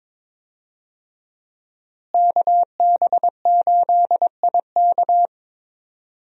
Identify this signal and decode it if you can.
Morse code, heard as a keyed tone, about 22 wpm, 700 Hz. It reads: KB8IK
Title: USA Call Signs - 5 Characters - Repeated 3x - 22wpm